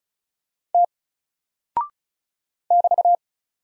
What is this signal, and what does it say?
Morse code, heard as a keyed tone, about 36 wpm, 700 Hz. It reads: T <BT>
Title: Single Character (Letter:Number:Punctuation:Prosign) 35wpm